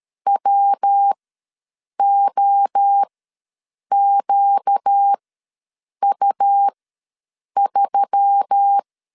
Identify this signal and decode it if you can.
Morse code, heard as a keyed tone, about 13 wpm, spaced as for 7 wpm, 800 Hz. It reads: WOQU3
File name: LEZIONI / 5 LBVC / 5 L ws 12wpm 2958